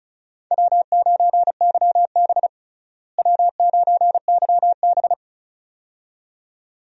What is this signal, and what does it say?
Morse code, heard as a keyed tone, about 34 wpm, 700 Hz. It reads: W9YB W9YB
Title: USA Call Signs - 4 Characters - Repeated 2x - 35wpm